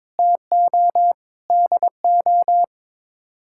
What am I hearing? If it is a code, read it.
Morse code, heard as a keyed tone, about 22 wpm, 700 Hz. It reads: TO DO
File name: practice / 22wpm / Sets of 6 Words - 22wpm